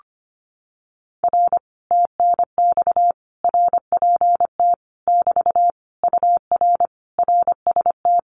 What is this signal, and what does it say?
Morse code, heard as a keyed tone, about 25 wpm, 700 Hz. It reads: R TNX RPT <BT> UR RST
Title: Common QSO Phrases Encore 25wpm